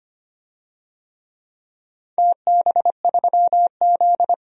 Morse code, heard as a keyed tone, about 25 wpm, 700 Hz. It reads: TB3Z
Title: DXCC-100 Call Signs - 4 Characters - 25wpm